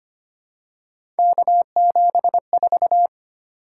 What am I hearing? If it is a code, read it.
Morse code, heard as a keyed tone, about 25 wpm, 700 Hz. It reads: K74
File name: Three Letter-Number Combinations - 25wpm